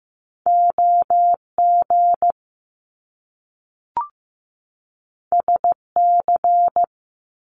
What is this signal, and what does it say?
Morse code, heard as a keyed tone, about 15 wpm, 700 Hz. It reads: OG SC
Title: Top Two Letter Combinations in Words 15wpm